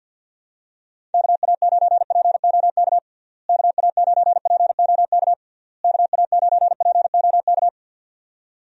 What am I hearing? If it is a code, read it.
Morse code, heard as a keyed tone, about 49 wpm, 700 Hz. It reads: KA9POK KA9POK KA9POK
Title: USA Call Signs - 6 Characters - Repeated 3x - 50wpm